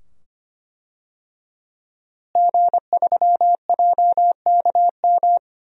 Morse code, heard as a keyed tone, about 25 wpm, 700 Hz. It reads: G3JKM
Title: DXCC-100 Call Signs - 5 Characters - Repeated 2x - 25wpm